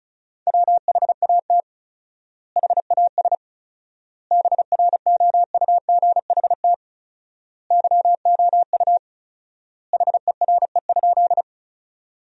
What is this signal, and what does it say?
Morse code, heard as a keyed tone, about 34 wpm, 700 Hz. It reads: WHAT HAS BROUGHT YOU HERE?
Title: 4x Word-Spacing - Sentences from Top 500 Words - 35wpm